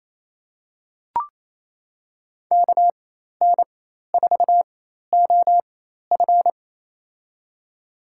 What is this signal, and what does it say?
Morse code, heard as a keyed tone, about 27 wpm, spaced as for 13 wpm, 700 Hz. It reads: KN4OF
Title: USA Call Signs - 5 Characters - 4x Character Spacing - 28wpm